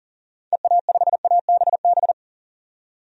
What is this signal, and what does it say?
Morse code, heard as a keyed tone, about 40 wpm, 700 Hz. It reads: EA5ABB
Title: DXCC-100 Call Signs - 6 Characters - 40wpm